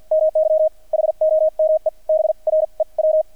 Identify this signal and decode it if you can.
Morse code, heard as a keyed tone, about 48 wpm, spaced as for 39 wpm, 630 Hz. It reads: MY HOME BREW
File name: archived / CRSnet